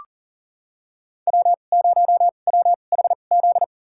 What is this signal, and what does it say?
Morse code, heard as a keyed tone, about 39 wpm, spaced as for 28 wpm, 700 Hz. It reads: W0WHZ
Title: USA Call Signs - 5 Characters - 2x Character Spacing - 40wpm